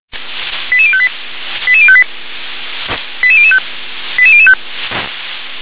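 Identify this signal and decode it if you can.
Five-tone selective call (ZVEI), heard as quick runs of five tones, tones of about 70 ms. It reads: 88058 88058 88005 88005
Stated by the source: ZVEI five-tone call 88912 received by SDR.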